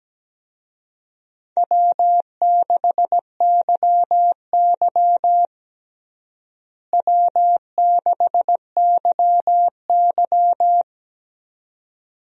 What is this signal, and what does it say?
Morse code, heard as a keyed tone, about 17 wpm, 700 Hz. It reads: W6YY W6YY
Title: USA Call Signs - 4 Characters - Repeated 2x - 17wpm